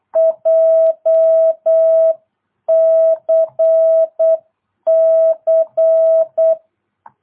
Morse code, heard as a keyed tone, about 8 wpm, 650 Hz. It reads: JCC